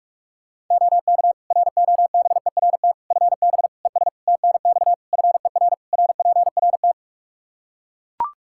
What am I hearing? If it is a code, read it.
Morse code, heard as a keyed tone, about 41 wpm, 700 Hz. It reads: OK ROBERT FB ES TNX FER RPRT
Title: K1USN SST QSO Practice 45wpm